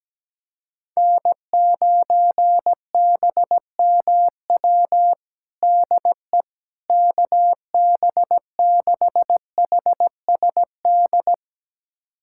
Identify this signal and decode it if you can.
Morse code, heard as a keyed tone, about 17 wpm, 700 Hz. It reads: N9BMW DE KB6HSD
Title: USA Callsign DE Callsign 17wpm